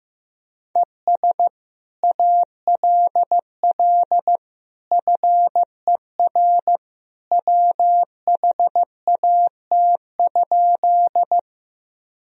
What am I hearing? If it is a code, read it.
Morse code, heard as a keyed tone, about 16 wpm, 700 Hz. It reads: ES ALL FER WHAT?
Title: Sentences from Top 100 Words and Abbreviations 15wpm